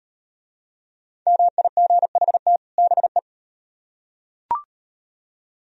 Morse code, heard as a keyed tone, about 35 wpm, 700 Hz. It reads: MIGHT BE